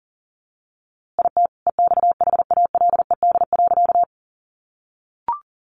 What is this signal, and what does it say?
Morse code, heard as a keyed tone, about 40 wpm, 700 Hz. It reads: IT EXHALED.
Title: A Sound Of Thunder 40wpm